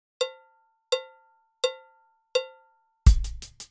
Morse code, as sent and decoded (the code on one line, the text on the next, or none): none